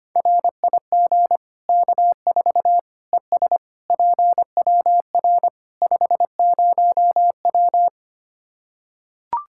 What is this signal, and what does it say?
Morse code, heard as a keyed tone, about 25 wpm, 700 Hz. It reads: RIG K4 ES PWR 50W
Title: Practice QSO - Exchange 2 - Info - 25wpm